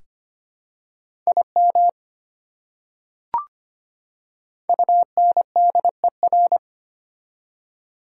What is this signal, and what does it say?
Morse code, heard as a keyed tone, about 25 wpm, 700 Hz. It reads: IM UNDER
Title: Prefixes and Suffixes 25wpm